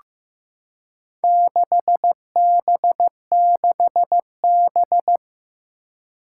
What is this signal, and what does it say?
Morse code, heard as a keyed tone, about 15 wpm, 700 Hz. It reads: 6B6B
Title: Four Challenging Letter-Number Combinations 15wpm